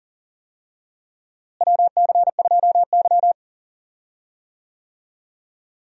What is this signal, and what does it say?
Morse code, heard as a keyed tone, about 40 wpm, 700 Hz. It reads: WC2Y
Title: USA Call Signs - 4 Characters - Repeated 2x - 40wpm